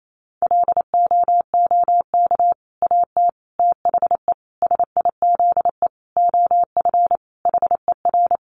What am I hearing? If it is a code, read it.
Morse code, heard as a keyed tone, about 28 wpm, 700 Hz. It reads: LOOK AT THE SIZE OF HER
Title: Sentences from Top 1200 Words 28wpm